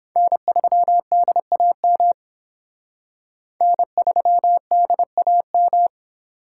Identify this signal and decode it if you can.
Morse code, heard as a keyed tone, about 28 wpm, 700 Hz. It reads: N3DAM N3DAM